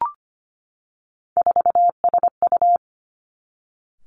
Morse code, heard as a keyed tone, about 25 wpm, 700 Hz. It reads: 4SU
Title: Three Challenging Letter-Number Combinations 25wpm